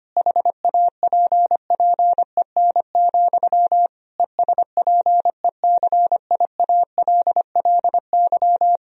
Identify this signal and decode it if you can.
Morse code, heard as a keyed tone, about 25 wpm, 700 Hz. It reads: HAPPEN, ESPECIALLY
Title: Sentences from Top 1000 Words - Review Part 4 - 25wpm